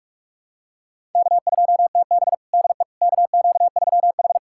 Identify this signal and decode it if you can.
Morse code, heard as a keyed tone, about 46 wpm, 700 Hz. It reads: K2TB DE KQ3H
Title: USA Callsign DE Callsign - 4 Characters - 45wpm